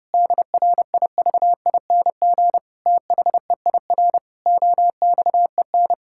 Morse code, heard as a keyed tone, about 30 wpm, 700 Hz. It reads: DRIVING THEIR OXEN
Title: Pangram Challenge v2 - 30wpm